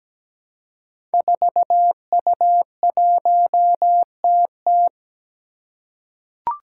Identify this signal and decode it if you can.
Morse code, heard as a keyed tone, about 17 wpm, 700 Hz. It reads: 4U1TT
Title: Five Challenging Letter-Number Combinations 17wpm